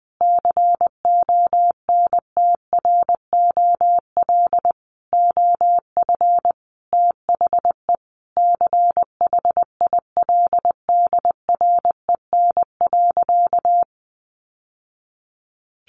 Morse code, heard as a keyed tone, about 20 wpm, 700 Hz. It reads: CONTROL OF THE CHILDREN.
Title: Sentences from Top 1200 Words 20wpm